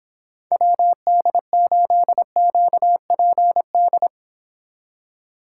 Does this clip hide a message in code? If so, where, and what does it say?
Yes: Morse code from 0.5 to 4.1 s, heard as a keyed tone, about 31 wpm, 700 Hz. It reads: WD8QPD